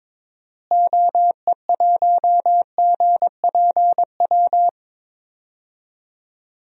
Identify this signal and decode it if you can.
Morse code, heard as a keyed tone, about 21 wpm, 700 Hz. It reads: OE1GPW
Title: DXCC-100 Call Signs - 6 Characters - Repeated 3x - 22wpm